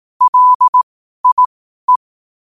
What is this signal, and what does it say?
Morse code, heard as a keyed tone, about 18 wpm, spaced as for 12 wpm, 1000 Hz. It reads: LIE
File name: qotdfiles / 12wpm